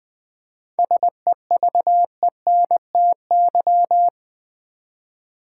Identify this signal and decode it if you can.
Morse code, heard as a keyed tone, about 20 wpm, 700 Hz. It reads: SEVENTY
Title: 7 Letter Words - First 3 Letters Given - 20wpm